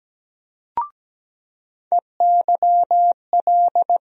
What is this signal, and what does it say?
Morse code, heard as a keyed tone, about 17 wpm, 700 Hz. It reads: EYL
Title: USA Call Sign Postfixes - 3 Letters 17wpm